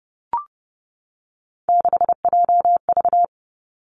Morse code, heard as a keyed tone, about 30 wpm, 700 Hz. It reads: 6JV
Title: Three Challenging Letter-Number Combinations 30wpm